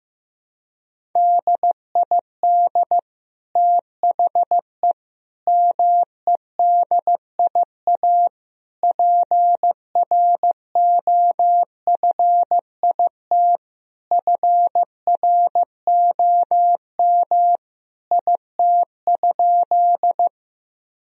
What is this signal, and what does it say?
Morse code, heard as a keyed tone, about 15 wpm, 700 Hz. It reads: DID THE MEDIA PROFIT FROM IT?
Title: Sentences from Top 1000 Words - Review Part 4 - 15wpm